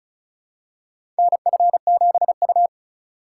Morse code, heard as a keyed tone, about 36 wpm, 700 Hz. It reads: NF7U
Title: USA Call Signs - 2x Character Spacing - 35wpm